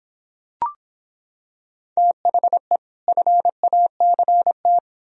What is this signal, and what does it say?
Morse code, heard as a keyed tone, about 25 wpm, 700 Hz. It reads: THE FACT